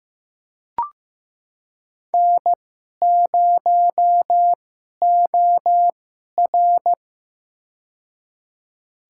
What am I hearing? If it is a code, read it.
Morse code, heard as a keyed tone, about 15 wpm, spaced as for 11 wpm, 700 Hz. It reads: N0OR
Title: USA Call Signs - 4 Characters - 2x Character Spacing - 15wpm